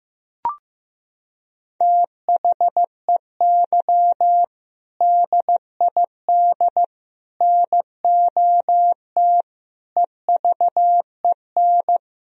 Morse code, heard as a keyed tone, about 15 wpm, 700 Hz. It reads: THEY DID NOT EVEN
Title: Sentences from Top 1000 Words - Review Part 1 - 15wpm